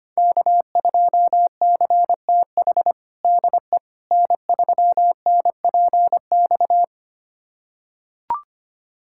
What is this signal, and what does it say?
Morse code, heard as a keyed tone, about 25 wpm, 700 Hz. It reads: K2CTH DE N3NPX
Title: USA Callsign DE Callsign - 5 Characters - 25wpm